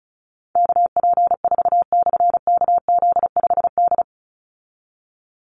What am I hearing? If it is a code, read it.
Morse code, heard as a keyed tone, about 34 wpm, 700 Hz. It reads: KP4/KZ5D
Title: Call Signs Repeated 2x - 35wpm